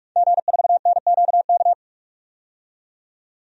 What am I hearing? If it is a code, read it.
Morse code, heard as a keyed tone, about 46 wpm, 700 Hz. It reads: G4NQK
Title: DXCC-100 Call Signs - 5 Characters - 45wpm